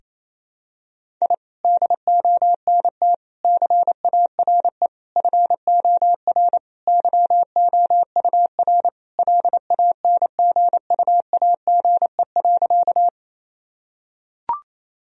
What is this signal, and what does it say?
Morse code, heard as a keyed tone, about 28 wpm, 700 Hz. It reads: I DONT CARE FOR YOUR LANGUAGE.
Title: Sentences from Top 1000 Words - Review Part 3 - 28wpm